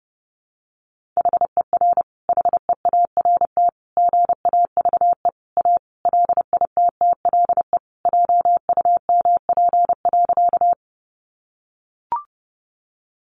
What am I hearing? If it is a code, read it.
Morse code, heard as a keyed tone, about 30 wpm, 700 Hz. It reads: HER HEART GAVE A LITTLE JUMP.
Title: Sentences from Top 1300 Words 30wpm